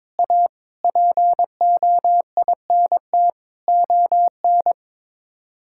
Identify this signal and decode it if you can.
Morse code, heard as a keyed tone, about 22 wpm, 700 Hz. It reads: A POINT ON
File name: practice / 22wpm / Sets of 8 Words - 22wpm